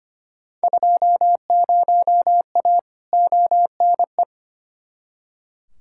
Morse code, heard as a keyed tone, about 25 wpm, 700 Hz. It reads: 20A ONE
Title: Field Day Exchanges Encore - 25wpm